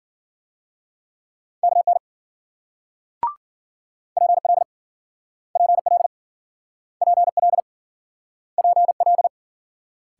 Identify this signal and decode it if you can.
Morse code, heard as a keyed tone, about 53 wpm, 700 Hz. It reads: KN PL PL PL PL